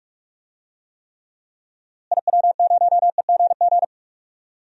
Morse code, heard as a keyed tone, about 41 wpm, 700 Hz. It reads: IW0EGG
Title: DXCC-100 Call Signs - 6 Characters - Repeated 2x - 45wpm